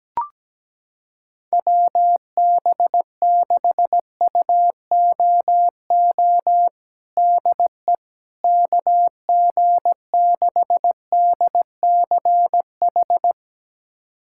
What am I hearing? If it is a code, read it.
Morse code, heard as a keyed tone, about 17 wpm, 700 Hz. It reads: WB6UOO DE KG6DCH
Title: USA Callsign DE Callsign 17wpm